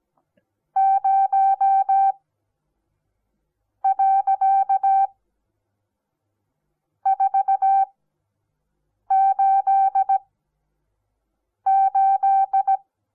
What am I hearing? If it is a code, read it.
Morse code, heard as a keyed tone, about 17 wpm, spaced as for 5 wpm, 780 Hz. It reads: 0.488
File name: Onlinelehrgang / morsen / DJ4UF Morsen08